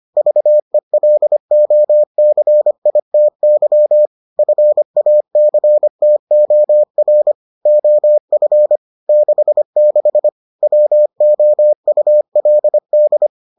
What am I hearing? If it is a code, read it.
Morse code, heard as a keyed tone, about 25 wpm, 600 Hz. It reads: VELOCITY FACTOR OF 66 WOULD